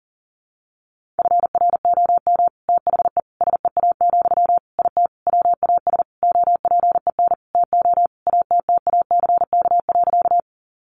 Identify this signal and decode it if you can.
Morse code, heard as a keyed tone, about 40 wpm, 700 Hz. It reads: FROM THE SEA, IT WAS OPEN TO ATTACK.
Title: Sentences from Top 1500 Words 40wpm